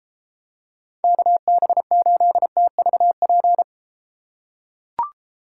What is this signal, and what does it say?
Morse code, heard as a keyed tone, about 33 wpm, 700 Hz. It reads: KB8TVP